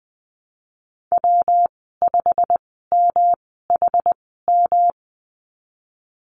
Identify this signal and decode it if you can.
Morse code, heard as a keyed tone, about 20 wpm, spaced as for 15 wpm, 700 Hz. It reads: W5MHM
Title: 161 Lesson 26g - Letter g - Call Signs 20wpm